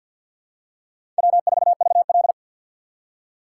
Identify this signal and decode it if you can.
Morse code, heard as a keyed tone, about 53 wpm, 700 Hz. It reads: W4VL
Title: USA Call Signs - 4 Characters - 6x Character Spacing - 50wpm